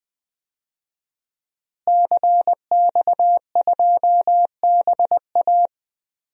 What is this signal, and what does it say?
Morse code, heard as a keyed tone, about 20 wpm, 700 Hz. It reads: CX2BA
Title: DXCC-100 Call Signs - 5 Characters - Repeated 2x - 20wpm